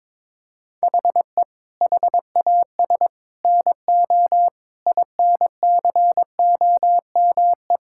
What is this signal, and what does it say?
Morse code, heard as a keyed tone, about 22 wpm, 700 Hz. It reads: HE HAS NO INCOME
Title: Sentences from Top 1000 Words - Review Part 5 - 22wpm